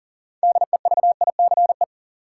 Morse code, heard as a keyed tone, about 40 wpm, 700 Hz. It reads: DEVICE
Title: Sentences from Top 1500 Words - New Word Drill - 40wpm